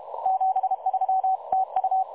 Morse code, heard as a keyed tone, about 33 wpm, 760 Hz. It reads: 73 TU
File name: dlazaawansowanych / nagraniazpasma / websdr recording 3565.0kHz